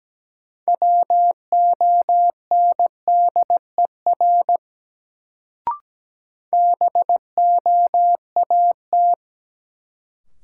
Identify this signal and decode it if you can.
Morse code, heard as a keyed tone, about 17 wpm, 700 Hz. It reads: WONDER BOAT
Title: Sentences from Top 500 Words - New Word Drill - 17wpm